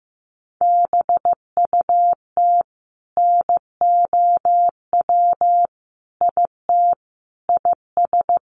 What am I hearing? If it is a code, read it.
Morse code, heard as a keyed tone, about 15 wpm, 700 Hz. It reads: BUT NOW IT IS
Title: Sentences from Top 600 Words 15wpm